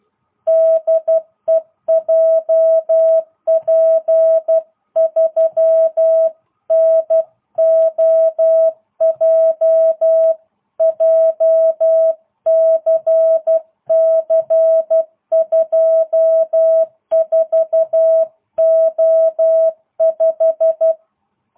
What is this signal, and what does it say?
Morse code, heard as a keyed tone, about 12 wpm, 660 Hz. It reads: DE JP3NOJ JCC 24O5